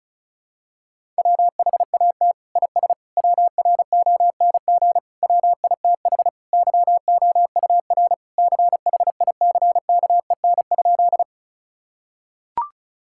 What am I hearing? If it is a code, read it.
Morse code, heard as a keyed tone, about 36 wpm, 700 Hz. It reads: WHAT IS WRONG WITH YOUR CHICKEN?
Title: Sentences from Top 1000 Words - Review Part 5 - 35wpm